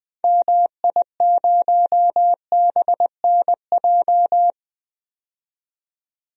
Morse code, heard as a keyed tone, about 20 wpm, 700 Hz. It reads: MI0BNJ
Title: DXCC-100 Call Signs - 6 Characters - 20wpm